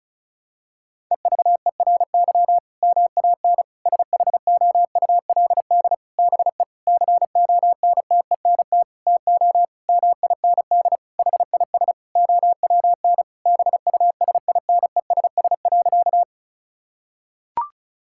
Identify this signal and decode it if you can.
Morse code, heard as a keyed tone, about 34 wpm, 700 Hz. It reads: EVERY MAN SHOULD BE CONTENT TO MIND HIS OWN BUSINESS.
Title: Sentences from Top 1000 Words - Review Part 6 - 35wpm